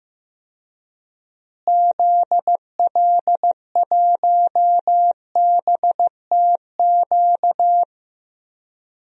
Morse code, heard as a keyed tone, about 15 wpm, 700 Hz. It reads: ZL1BTQ
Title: DXCC-100 Call Signs - 6 Characters - Repeated 2x - 15wpm